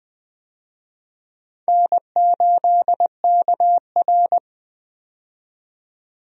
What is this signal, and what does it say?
Morse code, heard as a keyed tone, about 20 wpm, 700 Hz. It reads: N8KR
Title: USA Call Signs - 4 Characters - 20wpm